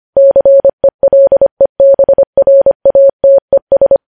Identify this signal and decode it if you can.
Morse code, heard as a keyed tone, about 25 wpm, 550 Hz. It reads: CELEBRATES
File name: podcast / News headlines-1x-25x25